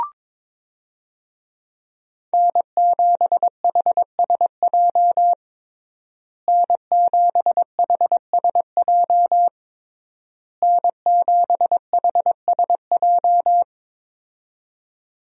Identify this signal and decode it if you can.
Morse code, heard as a keyed tone, about 22 wpm, 700 Hz. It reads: N7HSJ N7HSJ N7HSJ
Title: USA Call Signs - 5 Characters - Repeated 3x - 22wpm